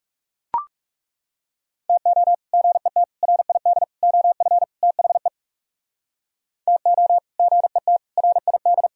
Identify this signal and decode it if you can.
Morse code, heard as a keyed tone, about 38 wpm, 700 Hz. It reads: TO GET RID OF THE TO GET RID
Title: Speed-Racing - Sets of 5 Words 30wpm